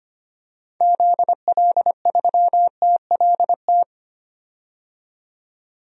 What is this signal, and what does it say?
Morse code, heard as a keyed tone, about 25 wpm, 700 Hz. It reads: ZL3TLT
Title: DXCC-100 Call Signs - 6 Characters - 25wpm